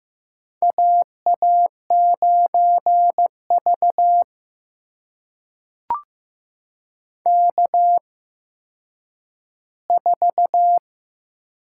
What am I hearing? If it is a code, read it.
Morse code, heard as a keyed tone, about 15 wpm, spaced as for 9 wpm, 700 Hz. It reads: AA9V K4
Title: USA Call Signs - 4 Characters - 8x Character Spacing - 15wpm